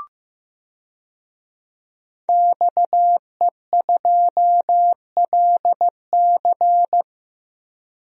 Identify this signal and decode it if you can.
Morse code, heard as a keyed tone, about 15 wpm, 700 Hz. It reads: XE2LC
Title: DXCC-100 Call Signs - 5 Characters - 15wpm